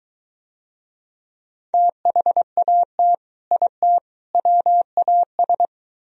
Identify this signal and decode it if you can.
Morse code, heard as a keyed tone, about 23 wpm, 700 Hz. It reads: THAT IT WAS